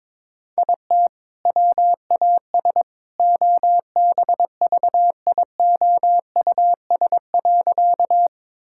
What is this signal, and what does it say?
Morse code, heard as a keyed tone, about 22 wpm, 700 Hz. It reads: IT WAS OBVIOUS.
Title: Sentences from Top 1600 Words 22wpm